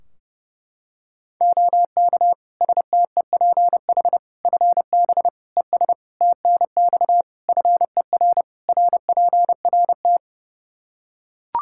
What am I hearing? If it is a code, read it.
Morse code, heard as a keyed tone, about 30 wpm, 700 Hz. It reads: OK STEPH FB ES TNX FER RPRT
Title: Practice QSO - Exchange 2 - Info - 30wpm